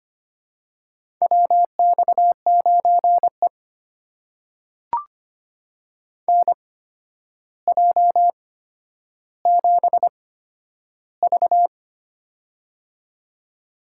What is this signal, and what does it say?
Morse code, heard as a keyed tone, about 25 wpm, spaced as for 11 wpm, 700 Hz. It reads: WX9E NJ7V
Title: USA Call Signs - 4 Characters - 8x Character Spacing - 25wpm